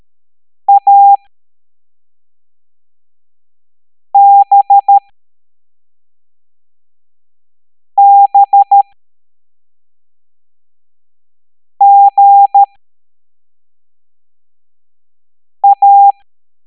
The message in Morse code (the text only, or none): ABBGA